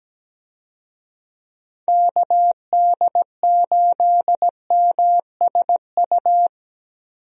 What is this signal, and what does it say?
Morse code, heard as a keyed tone, about 17 wpm, 700 Hz. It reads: KD8MSU
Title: USA Call Signs - 6 Characters - Repeated 3x - 17wpm